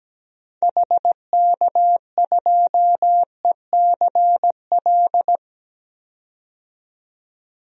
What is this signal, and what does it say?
Morse code, heard as a keyed tone, about 17 wpm, 700 Hz. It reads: HK2ECL
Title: DXCC-100 Call Signs - 6 Characters - 17wpm